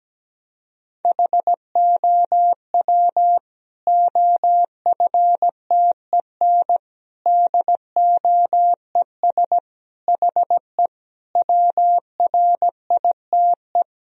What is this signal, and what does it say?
Morse code, heard as a keyed tone, about 17 wpm, 700 Hz. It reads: HOW OFTEN DOES HE WRITE
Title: Sentences from Top 1000 Words - Review Part 1 - 17wpm